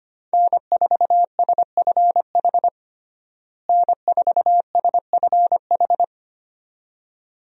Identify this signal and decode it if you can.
Morse code, heard as a keyed tone, about 25 wpm, 700 Hz. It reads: N4SFH N4SFH
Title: USA Call Signs - 5 Characters - Repeated 2x - 25wpm